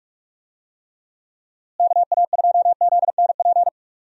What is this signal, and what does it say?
Morse code, heard as a keyed tone, about 46 wpm, 700 Hz. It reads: KA2ZNP
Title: USA Call Signs - 6 Characters - Repeated 2x - 45wpm